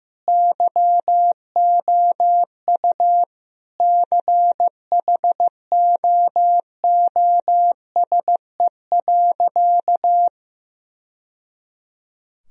Morse code, heard as a keyed tone, about 15 wpm, 700 Hz. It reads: YOU CHOOSE.
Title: Sentences from Top 1000 Words - Review Part 4 - 15wpm